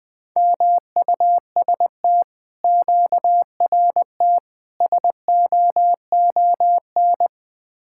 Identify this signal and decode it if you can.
Morse code, heard as a keyed tone, about 20 wpm, 700 Hz. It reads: MUST QRT SOON
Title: Speed-Racing Common QSO Phrases - 17wpm